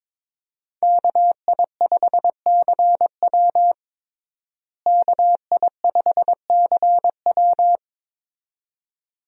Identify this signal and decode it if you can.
Morse code, heard as a keyed tone, about 22 wpm, 700 Hz. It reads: KI5CW KI5CW
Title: USA Call Signs - 5 Characters - Repeated 2x - 22wpm